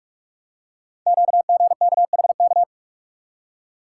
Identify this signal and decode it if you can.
Morse code, heard as a keyed tone, about 44 wpm, 700 Hz. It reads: QGKHK
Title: Code Groups - 5 Letters - 45wpm